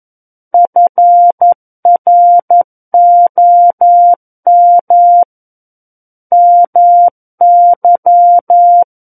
Morse code, heard as a keyed tone, about 11 wpm, 700 Hz. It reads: FROM MY
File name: 11 WPM